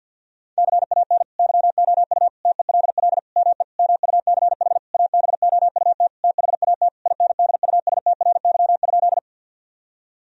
Morse code, heard as a keyed tone, about 50 wpm, 700 Hz. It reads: CAN YOU TELL ME MUCH ABOUT THAT INDUSTRY?
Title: Sentences from Top 1000 Words - Review Part 4 - 50wpm